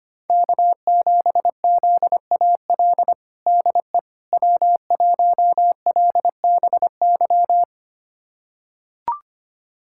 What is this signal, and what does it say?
Morse code, heard as a keyed tone, about 25 wpm, 700 Hz. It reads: K7ZAL DE W1LBY
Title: USA Callsign DE Callsign - 5 Characters - 25wpm